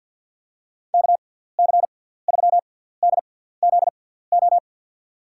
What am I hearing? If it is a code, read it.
Morse code, heard as a keyed tone, about 48 wpm, spaced as for 17 wpm, 700 Hz. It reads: KC3DZO
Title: USA Call Signs - 6 Characters - 6x Character Spacing - 50wpm